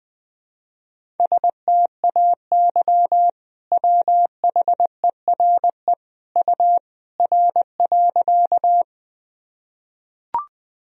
Morse code, heard as a keyed tone, about 21 wpm, 700 Hz. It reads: STAY WHERE U R.
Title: Sentences from Top 500 Words and Abbreviations 20wpm